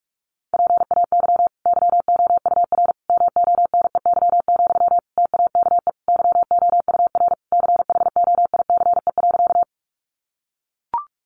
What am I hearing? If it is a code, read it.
Morse code, heard as a keyed tone, about 44 wpm, 700 Hz. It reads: PAY YOUR MONEY, TAKE YOUR CHOICE.
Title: Sentences from Top 900 Words 45wpm